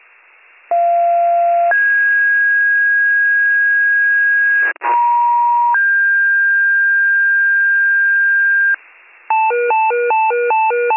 Two-tone page, heard as a one-second tone, then a three-second tone, about 690 then 1650 Hz, about 950 then 1600 Hz. It is heard from 0.7 to 8.7 s.